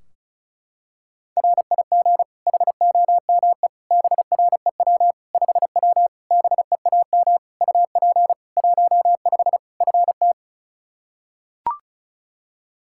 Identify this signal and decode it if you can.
Morse code, heard as a keyed tone, about 36 wpm, 700 Hz. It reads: RIG HOME BREW 5W BEAM UP 15 FT
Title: K1USN SST QSO Practice 35wpm